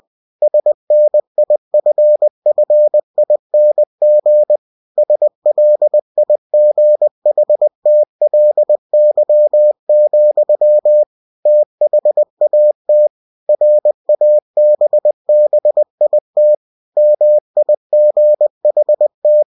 Morse code, heard as a keyed tone, about 20 wpm, 600 Hz. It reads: SNIFFING SLIGHTLY, THAT RABBIT MIGHT
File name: cw-books / winnie-the-pooh / winnie-20wpm-page-0017